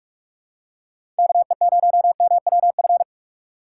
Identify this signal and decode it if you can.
Morse code, heard as a keyed tone, about 43 wpm, 700 Hz. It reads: KE0MWF
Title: USA Call Signs - 6 Characters - 4x Character Spacing - 45wpm